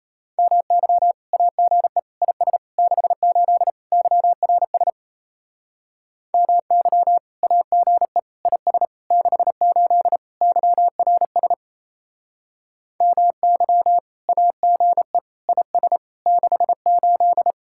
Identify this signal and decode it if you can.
Morse code, heard as a keyed tone, about 34 wpm, 700 Hz. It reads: MY AGE IS 68 YRS MY AGE IS 68 YRS MY AGE IS 68
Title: Speed-Racing Common QSO Phrases - 28wpm